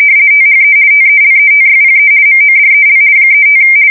RTTY, about 50 baud. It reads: THE TELETYPE MODEL 35
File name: fsk170-50B / teletype-corp